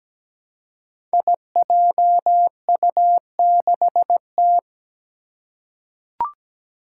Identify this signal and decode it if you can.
Morse code, heard as a keyed tone, about 17 wpm, 700 Hz. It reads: IJU6T
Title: Five Challenging Letter-Number Combinations 17wpm